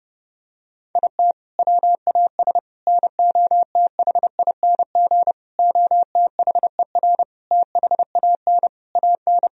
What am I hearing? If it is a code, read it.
Morse code, heard as a keyed tone, about 30 wpm, 700 Hz. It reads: IT WAS NOTHING OTHER THAN AN
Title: Sentences from Top 1000 Words - Review Part 2 - 30wpm